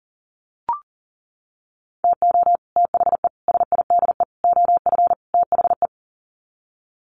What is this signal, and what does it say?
Morse code, heard as a keyed tone, about 40 wpm, 700 Hz. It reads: TO THE SIDE OF THE
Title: Sets of 5 Words 40wpm